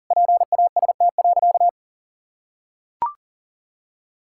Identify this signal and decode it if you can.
Morse code, heard as a keyed tone, about 37 wpm, 700 Hz. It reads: PAST.
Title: Sentences from Top 1000 Words - Review Part 6 - 40wpm